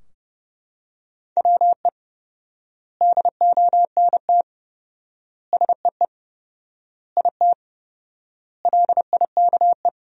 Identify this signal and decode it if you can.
Morse code, heard as a keyed tone, about 30 wpm, 700 Hz. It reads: WE DONT SEE IT LIKE
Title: 4x Word-Spacing - Sentences from Top 300 Words - 30wpm